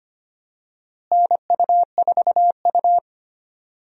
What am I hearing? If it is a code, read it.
Morse code, heard as a keyed tone, about 25 wpm, 700 Hz. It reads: NU4U
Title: USA Call Signs - 4 Characters - Repeated 3x - 25wpm